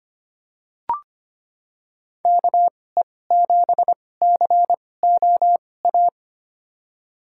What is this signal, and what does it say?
Morse code, heard as a keyed tone, about 25 wpm, spaced as for 18 wpm, 700 Hz. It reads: KE7COA
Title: USA Call Signs - 6 Characters - 2x Character Spacing - 25wpm